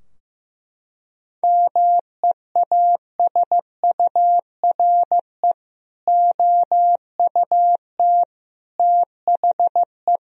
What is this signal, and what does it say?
Morse code, heard as a keyed tone, about 16 wpm, 700 Hz. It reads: MEASURE OUT THE
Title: Sentences from Top 1000 Words - Review Part 2 - 15wpm